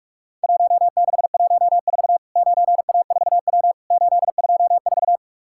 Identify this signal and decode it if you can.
Morse code, heard as a keyed tone, about 46 wpm, 700 Hz. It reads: 1614 9A4W 824
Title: Worked All Europe DX Contest - QTC Exchange - 45wpm